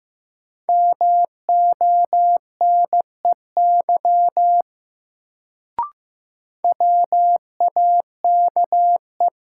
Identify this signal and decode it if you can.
Morse code, heard as a keyed tone, about 15 wpm, 700 Hz. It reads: MONEY WAKE
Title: Sets of 4 Words - New Word Drill - 15wpm